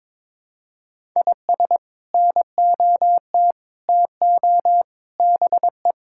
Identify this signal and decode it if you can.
Morse code, heard as a keyed tone, about 22 wpm, 700 Hz. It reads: IS NOT TO BE
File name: practice / 22wpm / Sets of 7 Words - 22wpm